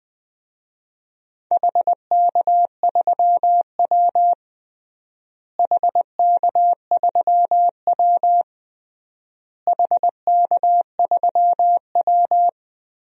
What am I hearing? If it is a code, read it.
Morse code, heard as a keyed tone, about 20 wpm, 700 Hz. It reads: HK3W HK3W HK3W
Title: DXCC-100 Call Signs - 4 Characters - Repeated 3x - 20wpm